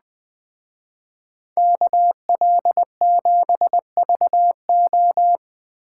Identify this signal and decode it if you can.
Morse code, heard as a keyed tone, about 20 wpm, 700 Hz. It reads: KL7VO
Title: Speed Racing - USA 2x2 Call Signs 17wpm